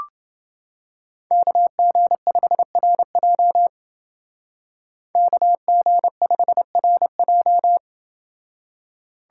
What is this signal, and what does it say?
Morse code, heard as a keyed tone, about 28 wpm, 700 Hz. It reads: KG5RJ KG5RJ